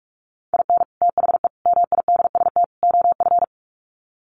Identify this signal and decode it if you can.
Morse code, heard as a keyed tone, about 44 wpm, 700 Hz. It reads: IN THE MIDST OF
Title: Sets of 4 Words 45wpm